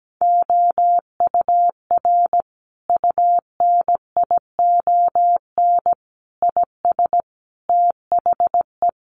Morse code, heard as a keyed tone, about 17 wpm, 700 Hz. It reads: OUR UNION IS THE
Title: Sentences from Top 1400 Words 17wpm